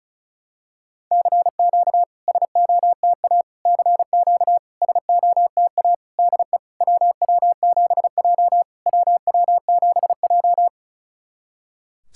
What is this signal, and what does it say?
Morse code, heard as a keyed tone, about 36 wpm, 690 Hz. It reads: CQ SOTA CQ SOTA DE WW7J WW7J
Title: SOTA QSO Practice 35wpm